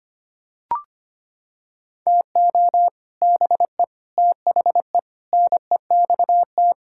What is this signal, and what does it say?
Morse code, heard as a keyed tone, about 25 wpm, 700 Hz. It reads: TO BE THE NEXT
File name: Sets of 9 Words - 25wpm